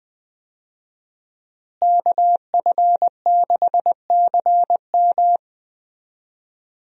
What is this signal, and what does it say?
Morse code, heard as a keyed tone, about 20 wpm, 700 Hz. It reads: KF6CM
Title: USA Call Signs - 5 Characters - Repeated 2x - 20wpm